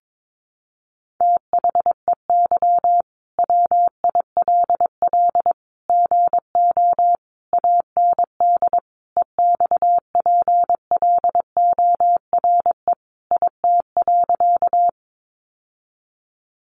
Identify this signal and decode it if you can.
Morse code, heard as a keyed tone, about 22 wpm, 700 Hz. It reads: THEY WILL GO AND EXPLORE IT.
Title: Sentences from Top 1100 Words 22wpm